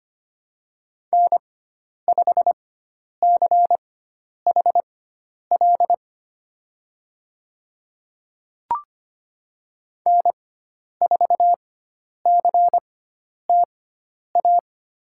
Morse code, heard as a keyed tone, about 25 wpm, spaced as for 10 wpm, 700 Hz. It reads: N5CHL N4CTA
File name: Lesson 16g - Letter c - Call Signs 25wpm